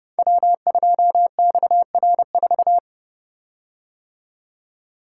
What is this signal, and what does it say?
Morse code, heard as a keyed tone, about 30 wpm, 700 Hz. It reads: W2XR4
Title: Code Groups - 5 Letter:Numbers - 30wpm